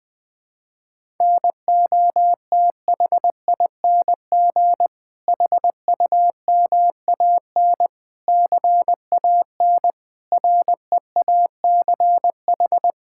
Morse code, heard as a keyed tone, about 20 wpm, 700 Hz. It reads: NOTHING HUMAN CAN REACH
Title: Sentences from Top 1600 Words 20wpm